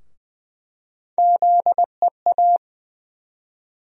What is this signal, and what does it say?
Morse code, heard as a keyed tone, about 20 wpm, 700 Hz. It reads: ZEA
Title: USA Call Sign Postfixes - 3 Letters 20wpm